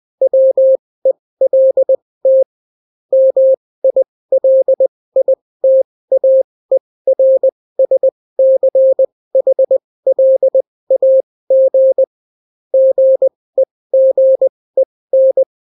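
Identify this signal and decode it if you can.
Morse code, heard as a keyed tone, about 20 wpm, spaced as for 16 wpm, 530 Hz. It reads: WELT MILITAERSCHLAG GEGEN
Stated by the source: Daily Morse weather and top stories at 16 wpm, clean without QRN.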